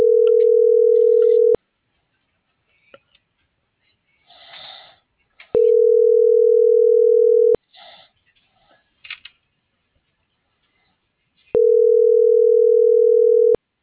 Ringback tone (North America), 440 + 480 Hz.